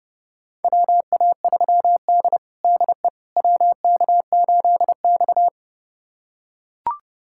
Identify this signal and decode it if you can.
Morse code, heard as a keyed tone, about 32 wpm, 700 Hz. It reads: WA3D DE WK8X
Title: USA Callsign DE Callsign - 4 Characters - 30wpm